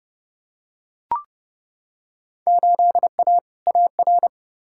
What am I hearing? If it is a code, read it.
Morse code, heard as a keyed tone, about 32 wpm, 700 Hz. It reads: 8A AR
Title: Field Day Exchanges Encore - 30wpm